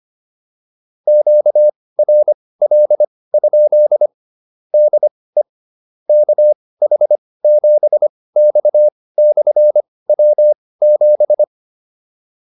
Morse code, heard as a keyed tone, about 25 wpm, spaced as for 18 wpm, 600 Hz. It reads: QRL? DE KH7X/W7
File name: QSO104 18